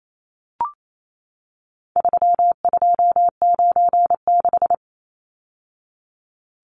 Morse code, heard as a keyed tone, about 28 wpm, 700 Hz. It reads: 3296
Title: Numbers - 4-Digit 28wpm